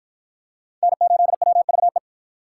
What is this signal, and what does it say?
Morse code, heard as a keyed tone, about 53 wpm, 700 Hz. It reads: N8WVE